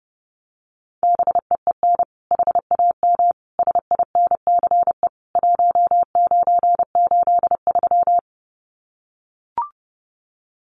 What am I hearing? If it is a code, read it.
Morse code, heard as a keyed tone, about 30 wpm, 700 Hz. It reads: BEEN HAM SINCE 1983
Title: Common QSO Phrases 30wpm